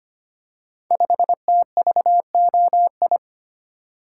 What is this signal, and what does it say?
Morse code, heard as a keyed tone, about 25 wpm, 700 Hz. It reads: 5TVOI
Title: Code Groups - 5 Letter:Numbers - 25wpm